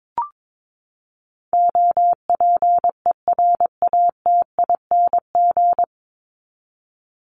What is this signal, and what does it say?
Morse code, heard as a keyed tone, about 22 wpm, 700 Hz. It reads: OPERATING
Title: ING Suffix - Top 100 Words with Base Words 22wpm